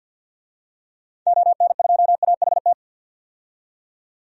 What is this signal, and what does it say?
Morse code, heard as a keyed tone, about 52 wpm, 700 Hz. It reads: ON2AHT
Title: DXCC-100 Call Signs - 6 Characters - 50wpm